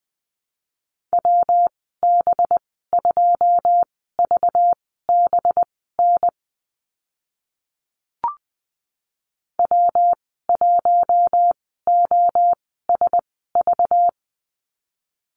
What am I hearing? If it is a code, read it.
Morse code, heard as a keyed tone, about 20 wpm, spaced as for 14 wpm, 700 Hz. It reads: WB2VBN W1OSV
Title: 193 Lesson 31g - Letter v - Call Signs 20wpm